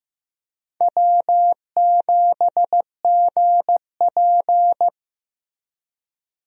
Morse code, heard as a keyed tone, about 15 wpm, 700 Hz. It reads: W7GP
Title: USA Call Signs - 4 Characters - Repeated 3x - 15wpm